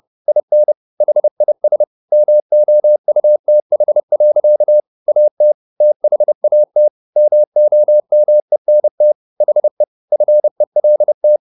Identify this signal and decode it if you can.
Morse code, heard as a keyed tone, about 30 wpm, 600 Hz. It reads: IN HIS MOUTH. AT THAT MOMENT HE FELT
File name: the old man and the sea / the old man and the sea 1 30wpm